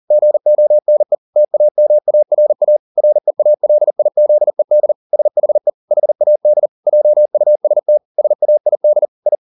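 Morse code, heard as a keyed tone, about 40 wpm, 600 Hz. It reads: GONE TAMARA REALIZED SHE HAD JUST SAID I
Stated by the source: Morse code at 40 wpm, practice text, CW med Gnister.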